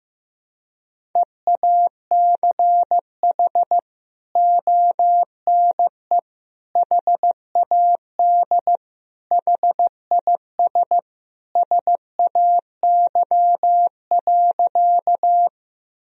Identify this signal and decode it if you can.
Morse code, heard as a keyed tone, about 15 wpm, 700 Hz. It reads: EACH ONE HAD HIS SAY.
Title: Sentences from Top 1000 Words - Review Part 1 - 15wpm